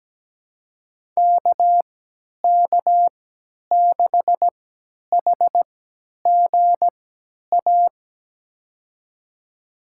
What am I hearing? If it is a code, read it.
Morse code, heard as a keyed tone, about 17 wpm, spaced as for 10 wpm, 700 Hz. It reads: KK6HGA
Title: USA Call Signs - 6 Characters - 3x Character Spacing - 17wpm